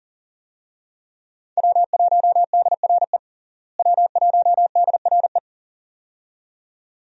Morse code, heard as a keyed tone, about 40 wpm, 700 Hz. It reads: W1DRE W1DRE
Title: USA Call Signs - 5 Characters - Repeated 2x - 40wpm